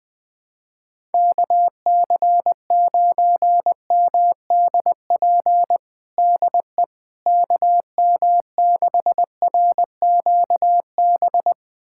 Morse code, heard as a keyed tone, about 21 wpm, 700 Hz. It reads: KC9MDP DE KM6RQB
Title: USA Callsign DE Callsign - 6 Characters - 20wpm